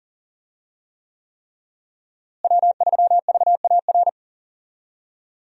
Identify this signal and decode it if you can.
Morse code, heard as a keyed tone, about 40 wpm, 700 Hz. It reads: W3VAR
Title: USA Call Signs - 5 Characters - Repeated 3x - 40wpm